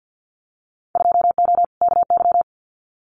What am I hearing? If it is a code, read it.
Morse code, heard as a keyed tone, about 48 wpm, 700 Hz. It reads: 2O KY
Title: Winter Field Day Exchanges 50wpm